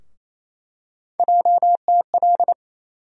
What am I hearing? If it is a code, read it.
Morse code, heard as a keyed tone, about 28 wpm, 700 Hz. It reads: JTL
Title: USA Call Sign Postfixes - 3 Letters 28wpm